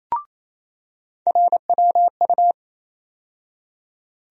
Morse code, heard as a keyed tone, about 28 wpm, 700 Hz. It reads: RWU
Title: USA Call Sign Postfixes - 3 Letters 28wpm